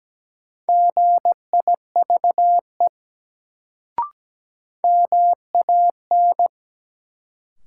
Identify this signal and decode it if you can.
Morse code, heard as a keyed tone, about 17 wpm, 700 Hz. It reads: GIVE MAN
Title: Sentences from Top 200 Words - New Word Drill - 17wpm